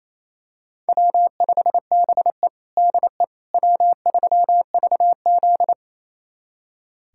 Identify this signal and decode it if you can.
Morse code, heard as a keyed tone, about 27 wpm, 700 Hz. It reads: W5BE DE W3VZ
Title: USA Callsign DE Callsign - 4 Characters - 28wpm